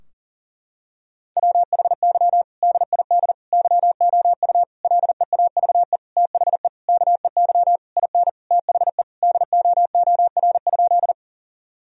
Morse code, heard as a keyed tone, about 37 wpm, 700 Hz. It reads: WHY DID YOU LEAVE THE KEY IN THE DOOR?
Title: Sentences from Top 1000 Words - Review Part 4 - 40wpm